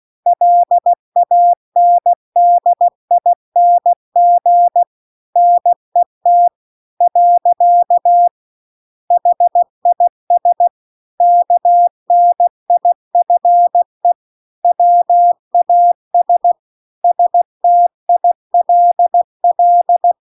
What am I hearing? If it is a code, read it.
Morse code, heard as a keyed tone, about 16 wpm, 700 Hz. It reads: LANDING NET . HIS KNIFE WAS STILL
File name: CW-The-Big-Twoheaded-River-2-16WPM